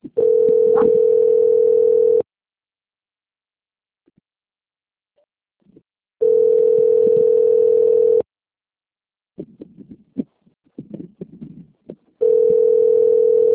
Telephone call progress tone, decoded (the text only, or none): ringback tone